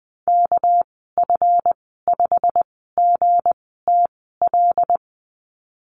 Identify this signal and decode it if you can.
Morse code, heard as a keyed tone, about 20 wpm, spaced as for 15 wpm, 700 Hz. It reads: KF5GTL
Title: 213 Lesson 34g - Number 8 - Call Signs 20wpm